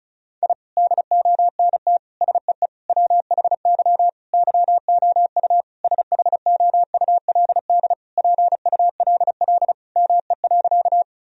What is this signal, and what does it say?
Morse code, heard as a keyed tone, about 36 wpm, 700 Hz. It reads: I DONT SEE WHY YOU SHOULD PULL ME.
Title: Sentences from Top 1000 Words - Review Part 2 - 35wpm